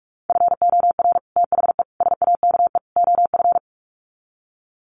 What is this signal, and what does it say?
Morse code, heard as a keyed tone, about 44 wpm, 700 Hz. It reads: FOR THE SAKE OF
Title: Sets of 4 Words 45wpm